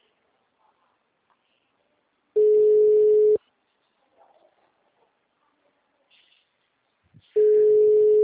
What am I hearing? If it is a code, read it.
Ringback tone (Europe, 425 Hz).